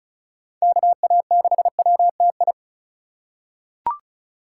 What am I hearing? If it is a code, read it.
Morse code, heard as a keyed tone, about 36 wpm, 700 Hz. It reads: KA6WTI
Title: USA Call Signs - 6 Characters - 3x Character Spacing - 35wpm